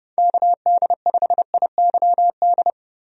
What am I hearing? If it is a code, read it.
Morse code, heard as a keyed tone, about 30 wpm, 700 Hz. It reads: KD5IYD
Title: USA Call Signs - 6 Characters - 30wpm